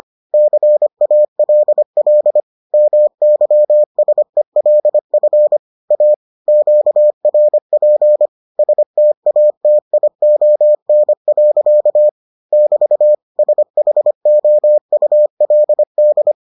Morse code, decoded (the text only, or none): CALL MYSELF A QRP STATION. <BT> SHOULD